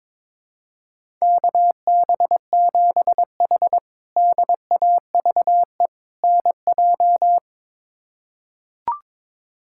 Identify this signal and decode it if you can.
Morse code, heard as a keyed tone, about 22 wpm, 700 Hz. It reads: KB7H DAVE NJ
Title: K1USN SST QSO Practice - 22wpm